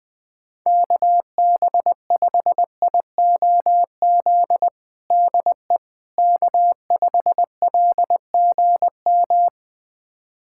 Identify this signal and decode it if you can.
Morse code, heard as a keyed tone, about 20 wpm, 700 Hz. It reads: KB5IOZ DE K5LGM
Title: USA Callsign DE Callsign 20wpm